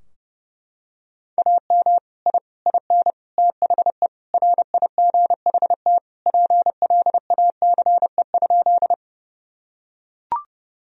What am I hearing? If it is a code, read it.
Morse code, heard as a keyed tone, about 30 wpm, 700 Hz. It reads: AM I IN THE RIGHT PLACE?
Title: Sentences from Top 1000 Words - Review Part 2 - 30wpm